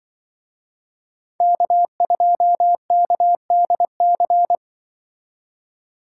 Morse code, heard as a keyed tone, about 25 wpm, 700 Hz. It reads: K2KDC
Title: Speed Racing - USA 1x3 Call Signs 20wpm